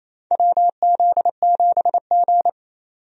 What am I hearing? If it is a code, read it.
Morse code, heard as a keyed tone, about 28 wpm, 700 Hz. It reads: WZ7G
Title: USA Call Signs - 4 Characters - 28wpm